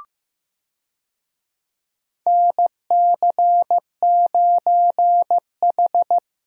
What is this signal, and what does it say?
Morse code, heard as a keyed tone, about 16 wpm, 700 Hz. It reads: NC9H
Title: USA Call Signs - 4 Characters - Repeated 3x - 15wpm